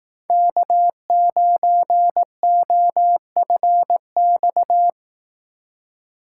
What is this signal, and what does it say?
Morse code, heard as a keyed tone, about 18 wpm, 700 Hz. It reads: K9OFX
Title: Speed Racing - USA 1x3 Call Signs 15wpm